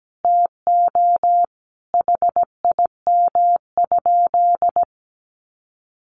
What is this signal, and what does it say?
Morse code, heard as a keyed tone, about 17 wpm, 700 Hz. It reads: TO HIM?
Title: Sentences from Top 700 Words 17wpm